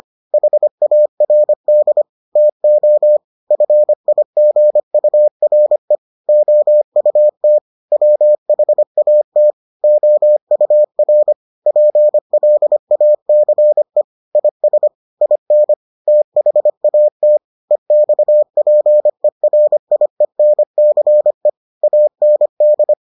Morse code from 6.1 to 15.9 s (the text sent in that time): OUT WHAT OUR PLACE IS IN